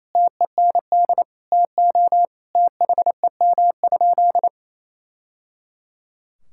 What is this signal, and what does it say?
Morse code, heard as a keyed tone, about 28 wpm, 700 Hz. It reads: TEND TO THEM?
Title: Sentences from Top 1000 Words - Review Part 5 - 28wpm